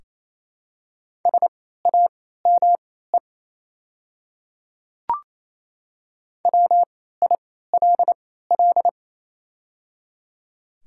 Morse code, heard as a keyed tone, about 28 wpm, spaced as for 16 wpm, 700 Hz. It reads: SAME WILL
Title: 3x Character Spacing - Top 300 Words - 28wpm